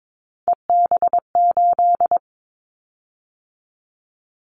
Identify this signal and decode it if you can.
Morse code, heard as a keyed tone, about 22 wpm, 700 Hz. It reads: EB8
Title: Callsign Prefixes for Top 100 DXCC - 22wpm